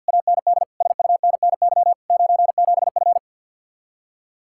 Morse code, heard as a keyed tone, about 64 wpm, 700 Hz. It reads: AND SUNNY 97F